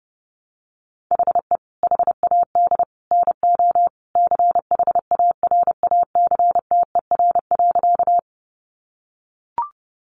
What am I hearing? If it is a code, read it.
Morse code, heard as a keyed tone, about 28 wpm, 700 Hz. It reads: HE HAD NO CHARACTER.
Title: Sentences from Top 1200 Words 30wpm